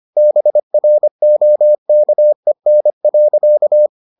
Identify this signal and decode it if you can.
Morse code, heard as a keyed tone, about 25 wpm, 600 Hz. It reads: BROKEN.